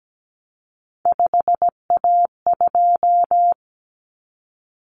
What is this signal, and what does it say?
Morse code, heard as a keyed tone, about 17 wpm, 700 Hz. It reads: 5A2
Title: Common Cut Numbers - 3-Digit 17wpm